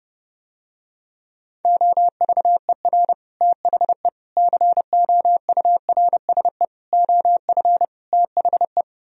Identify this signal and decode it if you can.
Morse code, heard as a keyed tone, about 30 wpm, 700 Hz. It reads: OVER THE COURSE OF THE